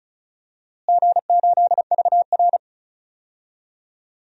Morse code, heard as a keyed tone, about 34 wpm, 700 Hz. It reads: G8VR
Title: DXCC-100 Call Signs - 4 Characters - Repeated 2x - 35wpm